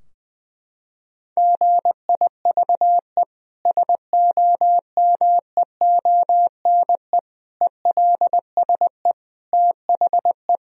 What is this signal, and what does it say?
Morse code, heard as a keyed tone, about 20 wpm, 700 Hz. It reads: GIVE SOMEONE ELSE THE
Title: Sentences from Top 1000 Words - Review Part 4 - 20wpm